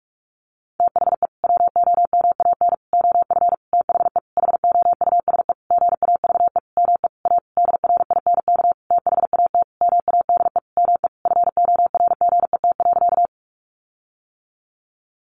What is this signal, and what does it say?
Morse code, heard as a keyed tone, about 44 wpm, 700 Hz. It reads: THE WOMAN OF THE HOUSE GAVE ME A DRINK THAT MADE ME FORGET.
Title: Sentences from Top 1500 Words 45wpm